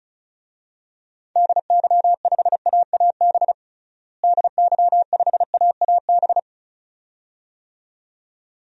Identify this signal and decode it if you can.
Morse code, heard as a keyed tone, about 36 wpm, 700 Hz. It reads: DY5AAB DY5AAB
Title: DXCC-100 Call Signs - 6 Characters - Repeated 2x - 35wpm